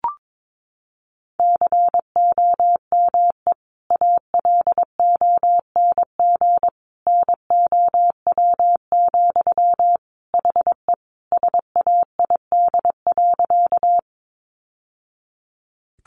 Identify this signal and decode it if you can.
Morse code, heard as a keyed tone, about 22 wpm, 700 Hz. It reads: COME ALONG NOW, HE SAID.
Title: Sentences from Top 700 Words 22wpm